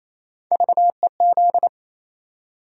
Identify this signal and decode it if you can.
Morse code, heard as a keyed tone, about 28 wpm, 700 Hz. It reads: VEZ
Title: USA Call Sign Postfixes - 3 Letters 28wpm